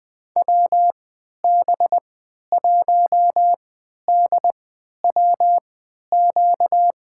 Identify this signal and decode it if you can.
Morse code, heard as a keyed tone, about 20 wpm, spaced as for 11 wpm, 700 Hz. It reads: WB1DWQ
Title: USA Call Signs - 6 Characters - 3x Character Spacing - 20wpm